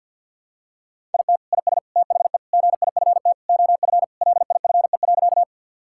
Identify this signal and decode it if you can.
Morse code, heard as a keyed tone, about 50 wpm, 700 Hz. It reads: IT IS THE GIFT OF LIFE.
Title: Sentences from Top 1000 Words - Review Part 6 - 50wpm